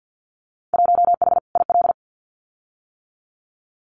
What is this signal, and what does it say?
Morse code, heard as a keyed tone, about 50 wpm, 700 Hz. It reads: 1H IL
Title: Winter Field Day Exchanges 50wpm